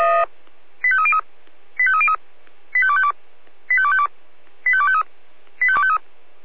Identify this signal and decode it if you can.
Five-tone selective call (ZVEI), heard as quick runs of five tones, tones of about 70 ms. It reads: 85303 85303 85252 85252 85253 85253